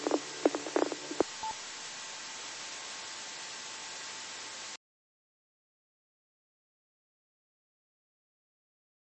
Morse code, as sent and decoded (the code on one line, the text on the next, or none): none